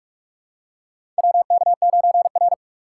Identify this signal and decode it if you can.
Morse code, heard as a keyed tone, about 44 wpm, 690 Hz. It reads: WK9R
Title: USA Call Signs - 4 Characters - 3x Character Spacing - 45wpm